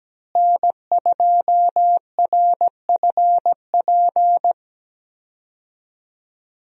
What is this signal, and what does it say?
Morse code, heard as a keyed tone, about 17 wpm, 700 Hz. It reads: N2RFP
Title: USA Call Signs - 5 Characters - Repeated 3x - 17wpm